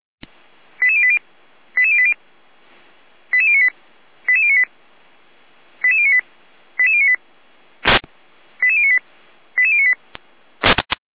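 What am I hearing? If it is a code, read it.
Five-tone selective call (ZVEI), heard as quick runs of five tones, tones of about 70 ms. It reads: 88080 88080 88098 88098 88098 88098 88098 88098
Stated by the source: ZVEI five-tone call received by SDR.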